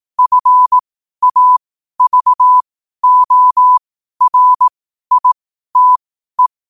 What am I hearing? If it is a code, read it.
Morse code, heard as a keyed tone, about 18 wpm, spaced as for 12 wpm, 1000 Hz. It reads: FAVORITE